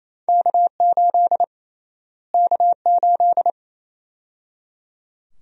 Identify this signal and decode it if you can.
Morse code, heard as a keyed tone, about 28 wpm, 700 Hz. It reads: K8 K8
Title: USA Call Sign Prefixes - Repeated 2x - 28wpm